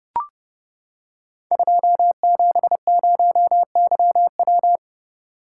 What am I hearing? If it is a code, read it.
Morse code, heard as a keyed tone, about 30 wpm, 700 Hz. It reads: 270YW
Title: Code Groups - 5 Letter:Numbers - 30wpm